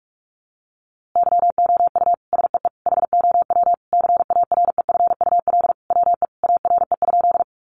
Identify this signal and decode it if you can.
Morse code, heard as a keyed tone, about 44 wpm, 700 Hz. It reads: YOU SEE HOW CAREFUL WE ARE?
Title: A Sound Of Thunder 45wpm